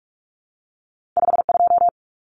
Morse code, heard as a keyed tone, about 46 wpm, 700 Hz. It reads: 52
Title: Common Cut Numbers - 2-Digit 45wpm